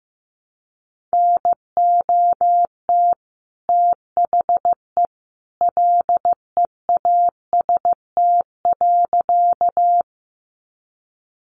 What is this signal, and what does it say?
Morse code, heard as a keyed tone, about 15 wpm, 700 Hz. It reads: NOT THE LEAST.
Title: Sentences from Top 700 Words 15wpm